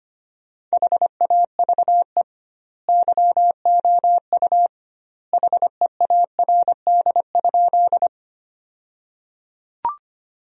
Morse code, heard as a keyed tone, about 25 wpm, 700 Hz. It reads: HAVE YOU HEARD?
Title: 2x Word-Spacing - Sentences from Top 400 Words - 25wpm